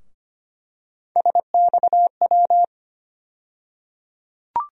Morse code, heard as a keyed tone, about 25 wpm, 700 Hz. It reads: SXW
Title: USA Call Sign Postfixes - 3 Letters 25wpm